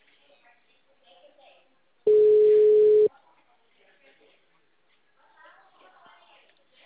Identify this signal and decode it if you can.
Ringback tone (Europe, 425 Hz), with speech.